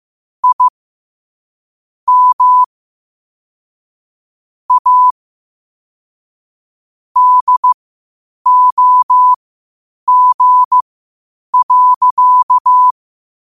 Morse code, heard as a keyed tone, about 15 wpm, spaced as for 8 wpm, 1000 Hz. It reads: I M A DOG.
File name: qotdfiles / 8wpm